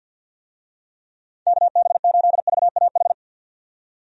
Morse code, heard as a keyed tone, about 53 wpm, 700 Hz. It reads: KB8VAH
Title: USA Call Signs - 6 Characters - 50wpm